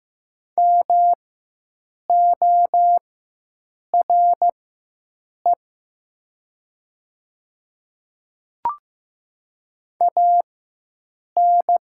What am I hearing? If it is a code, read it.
Morse code, heard as a keyed tone, about 15 wpm, spaced as for 7 wpm, 700 Hz. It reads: MORE AN
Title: 4x Character Spacing - Top 100 Words - 15wpm